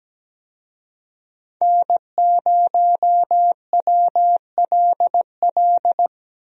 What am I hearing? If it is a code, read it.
Morse code, heard as a keyed tone, about 17 wpm, 700 Hz. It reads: N0WLL
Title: USA Call Signs - 5 Characters - 17wpm